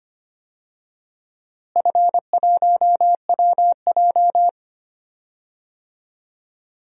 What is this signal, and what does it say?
Morse code, heard as a keyed tone, about 25 wpm, 700 Hz. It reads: F1WJ
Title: DXCC-100 Call Signs - 4 Characters - Repeated 3x - 25wpm